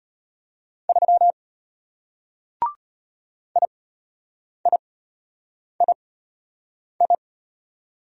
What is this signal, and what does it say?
Morse code, heard as a keyed tone, about 34 wpm, spaced as for 8 wpm, 700 Hz. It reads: 3 IIII